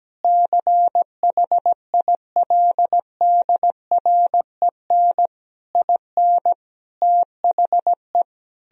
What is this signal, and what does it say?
Morse code, heard as a keyed tone, about 17 wpm, 700 Hz. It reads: CHILDREN IN THE
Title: Sentences from Top 1000 Words - Review Part 2 - 17wpm